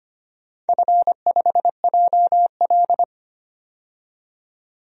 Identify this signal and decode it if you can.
Morse code, heard as a keyed tone, about 25 wpm, 700 Hz. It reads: F5JL
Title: DXCC-100 Call Signs - 4 Characters - 25wpm